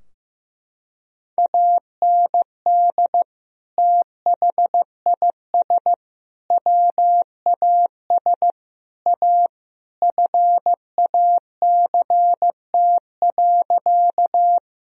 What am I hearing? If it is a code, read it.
Morse code, heard as a keyed tone, about 15 wpm, 700 Hz. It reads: AND THIS WAS A FACT.
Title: Sentences from Top 1000 Words - Review Part 3 - 15wpm